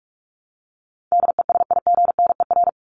Morse code, heard as a keyed tone, about 44 wpm, 700 Hz. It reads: DESIGNER
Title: Top 1501-1750 Words 45wpm